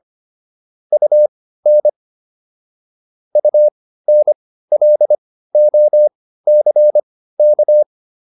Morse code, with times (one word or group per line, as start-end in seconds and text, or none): 0.9-1.9 UN
3.4-7.8 UNLOCK